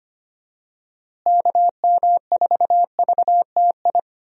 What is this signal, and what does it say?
Morse code, heard as a keyed tone, about 25 wpm, 700 Hz. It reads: KM4VTI
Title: USA Call Signs - 6 Characters - Repeated 2x - 25wpm